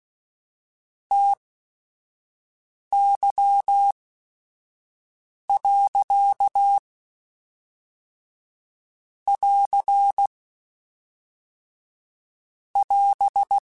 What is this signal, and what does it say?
Morse code, heard as a keyed tone, about 16 wpm, 800 Hz. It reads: T Y . <AR> <AS>